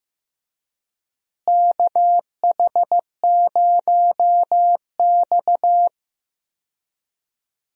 Morse code, heard as a keyed tone, about 15 wpm, 700 Hz. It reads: KH0X
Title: USA Call Signs - 4 Characters - Repeated 2x - 15wpm